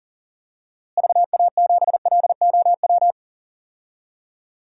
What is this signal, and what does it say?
Morse code, heard as a keyed tone, about 40 wpm, 700 Hz. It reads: VA7LOW
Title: DXCC-100 Call Signs - 6 Characters - Repeated 3x - 40wpm